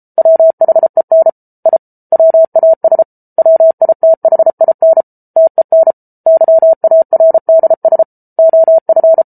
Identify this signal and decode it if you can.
Morse code, heard as a keyed tone, about 34 wpm, 650 Hz. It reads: WHEN I WAS WITHIN TEN YARDS OF